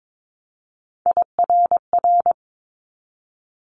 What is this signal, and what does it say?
Morse code, heard as a keyed tone, about 22 wpm, 700 Hz. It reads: IRR
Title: Multi-letter Phonemes 22wpm